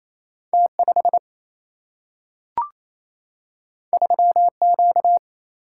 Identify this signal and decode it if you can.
Morse code, heard as a keyed tone, about 28 wpm, 700 Hz. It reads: T 5 3 Q
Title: Single-Letter Single-Number 28wpm